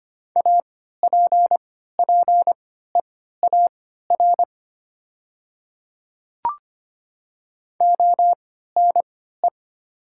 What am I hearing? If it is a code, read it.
Morse code, heard as a keyed tone, about 25 wpm, spaced as for 14 wpm, 690 Hz. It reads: APPEAR ONE
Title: 3x Character Spacing - Top 500 Words - 25wpm